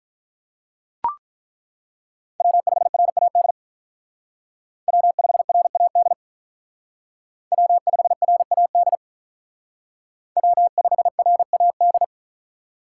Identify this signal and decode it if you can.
Morse code, heard as a keyed tone, about 42 wpm, 700 Hz. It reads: W5RAD W5RAD W5RAD W5RAD
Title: Speed Racing - USA 1x3 Call Signs 35wpm